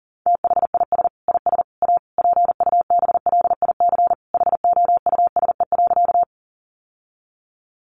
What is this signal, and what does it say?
Morse code, heard as a keyed tone, about 40 wpm, 700 Hz. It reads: THIS IS A PUBLIC HOUSE.
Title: Sentences from Top 600 Words 40wpm